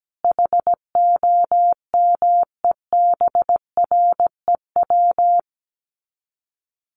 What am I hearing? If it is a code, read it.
Morse code, heard as a keyed tone, about 17 wpm, 700 Hz. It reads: HOMEBREW
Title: Common QSO Phrases 17wpm